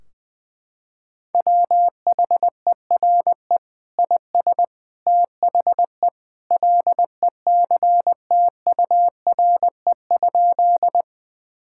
Morse code, heard as a keyed tone, about 20 wpm, 700 Hz. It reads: WHERE IS THE LECTURE?
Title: Sentences from Top 1600 Words 20wpm